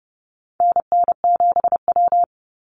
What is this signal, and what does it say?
Morse code, heard as a keyed tone, about 30 wpm, 700 Hz. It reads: NN7W
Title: Call Signs - 30wpm